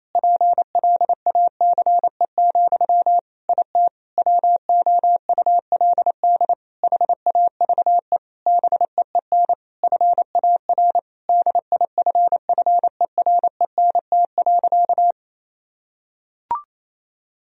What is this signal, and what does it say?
Morse code, heard as a keyed tone, about 28 wpm, 700 Hz. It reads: PLACE, IT WOULD HAVE BEEN FAR DIFFERENT.
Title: Sentences from Top 1000 Words - Review Part 3 - 28wpm